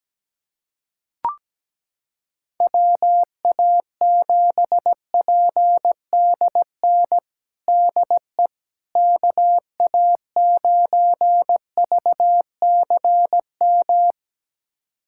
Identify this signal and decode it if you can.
Morse code, heard as a keyed tone, about 17 wpm, 700 Hz. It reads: WA7PDN DE KA9VCM
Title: USA Callsign DE Callsign 17wpm